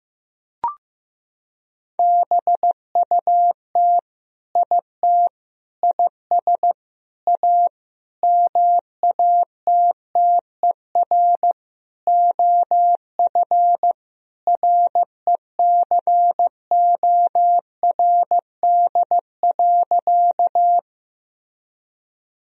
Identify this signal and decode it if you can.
Morse code, heard as a keyed tone, about 15 wpm, 700 Hz. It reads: BUT IT IS A MATTER OF RECORD.
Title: Sentences from Top 1000 Words - Review Part 3 - 15wpm